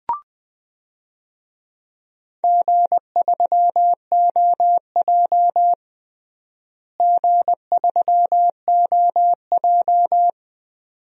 Morse code, heard as a keyed tone, about 20 wpm, 700 Hz. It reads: G3OJ G3OJ
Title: DXCC-100 Call Signs - 4 Characters - Repeated 2x - 20wpm